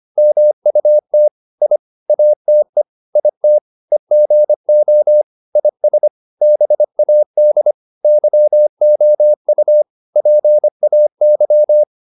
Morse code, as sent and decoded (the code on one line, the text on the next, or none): -- ..- - / .. / .- - . / .. - / . --. --- / .. ... / -... .- -.. / -.-- --- ..- / .--. .- -.--
MUT I ATE IT EGO IS BAD YOU PAY